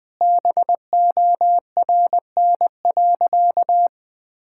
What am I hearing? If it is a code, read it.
Morse code, heard as a keyed tone, about 19 wpm, 700 Hz. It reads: BORN.
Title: Sentences from Top 1600 Words 20wpm